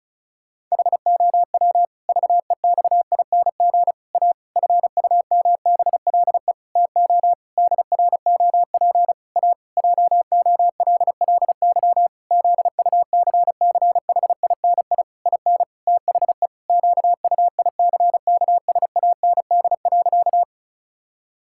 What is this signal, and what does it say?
Morse code, heard as a keyed tone, about 36 wpm, 700 Hz. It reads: HOW VEXING A FUMBLE TO DROP A JOLLY ZUCCHINI IN THE QUICKSAND.
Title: Pangram Challenge v2 - 35wpm